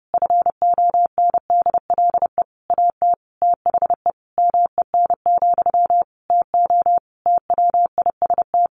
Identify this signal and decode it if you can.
Morse code, heard as a keyed tone, about 30 wpm, 700 Hz. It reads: FONDLE AT THE MEN, TO TWIST
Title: A Sound Of Thunder 30wpm